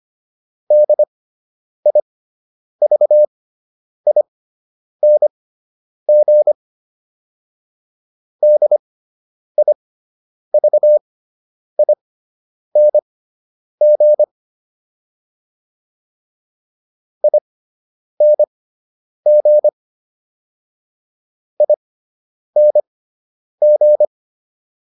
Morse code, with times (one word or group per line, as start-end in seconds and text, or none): 0.7-6.5 DIVING
8.4-14.2 DIVING
17.2-19.7 ING
21.6-24.1 ING